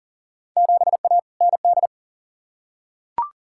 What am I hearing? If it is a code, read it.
Morse code, heard as a keyed tone, about 40 wpm, 700 Hz. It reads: 7A ND
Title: Field Day Exchanges Encore - 40wpm